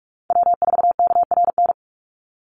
Morse code, heard as a keyed tone, about 44 wpm, 700 Hz. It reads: W4KRN
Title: NA SOTA Chaser Callsigns 45wpm